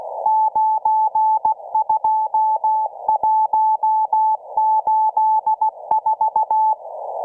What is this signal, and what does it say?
Morse code, heard as a keyed tone, about 16 wpm, 870 Hz. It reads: 92184